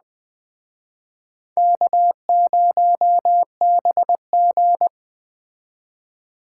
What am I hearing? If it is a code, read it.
Morse code, heard as a keyed tone, about 22 wpm, 700 Hz. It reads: K0BG